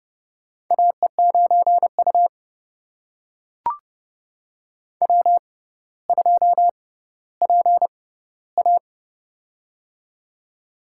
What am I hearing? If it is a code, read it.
Morse code, heard as a keyed tone, about 30 wpm, spaced as for 15 wpm, 700 Hz. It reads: AE9U W2PA
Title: USA Call Signs - 4 Characters - 6x Character Spacing - 30wpm